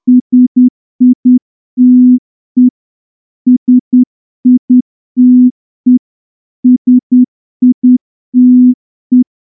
Morse code, heard as a keyed tone, about 9 wpm, 250 Hz. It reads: SITE SITE SITE
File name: morse-training-10WPM